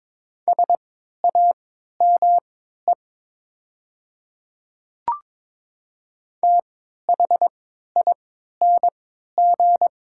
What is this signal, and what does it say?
Morse code, heard as a keyed tone, about 22 wpm, spaced as for 13 wpm, 700 Hz. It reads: SAME THING
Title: 3x Character Spacing - Top 200 Words - 22wpm